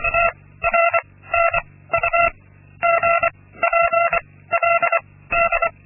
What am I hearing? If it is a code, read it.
Morse code, heard as a keyed tone, about 24 wpm, 680 Hz. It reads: T R N U G P L D
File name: Linksmakalnis-2